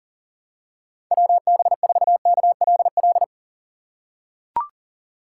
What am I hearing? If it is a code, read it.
Morse code, heard as a keyed tone, about 40 wpm, 700 Hz. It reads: WB4KLL
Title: USA Call Signs - 6 Characters - 3x Character Spacing - 40wpm